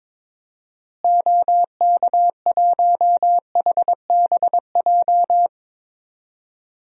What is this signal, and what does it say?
Morse code, heard as a keyed tone, about 22 wpm, 700 Hz. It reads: OK1HBJ
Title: DXCC-100 Call Signs - 6 Characters - Repeated 3x - 22wpm